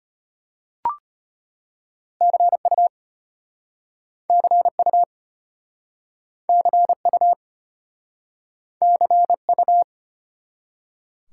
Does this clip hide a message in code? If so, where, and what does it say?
Yes: Morse code from 2.2 to 9.8 s, heard as a keyed tone, about 33 wpm, 700 Hz. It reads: CU CU CU CU